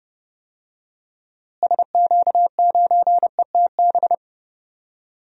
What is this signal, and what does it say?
Morse code, heard as a keyed tone, about 31 wpm, 700 Hz. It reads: SQ9ETB
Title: DXCC-100 Call Signs - 6 Characters - Repeated 2x - 30wpm